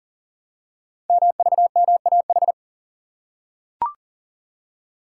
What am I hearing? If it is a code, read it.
Morse code, heard as a keyed tone, about 38 wpm, 700 Hz. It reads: MVMAH
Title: Code Groups - 5 Letters - 40wpm